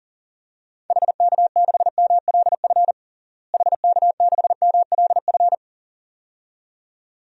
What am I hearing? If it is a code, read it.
Morse code, heard as a keyed tone, about 40 wpm, 700 Hz. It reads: HK6MLF HK6MLF
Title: DXCC-100 Call Signs - 6 Characters - Repeated 2x - 40wpm